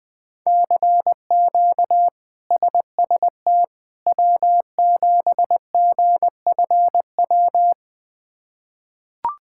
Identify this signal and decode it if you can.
Morse code, heard as a keyed tone, about 20 wpm, 700 Hz. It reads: CQ SST W7GFW
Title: K1USN SST QSO Practice - 20wpm